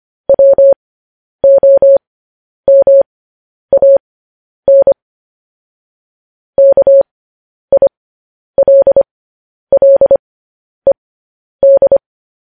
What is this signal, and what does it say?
Morse code, heard as a keyed tone, about 25 wpm, spaced as for 10 wpm, 550 Hz. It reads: WOMAN KILLED
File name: podcast / News headlines-1x-25x10